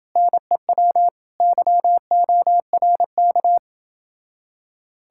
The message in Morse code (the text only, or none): NEW YORK